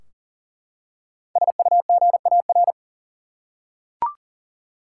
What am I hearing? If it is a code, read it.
Morse code, heard as a keyed tone, about 40 wpm, 700 Hz. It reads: SUGAR
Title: Five Letter Words - 40wpm